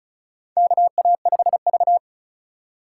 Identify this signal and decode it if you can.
Morse code, heard as a keyed tone, about 36 wpm, 700 Hz. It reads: KA5V
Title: USA Call Signs - 3x Character Spacing - 35wpm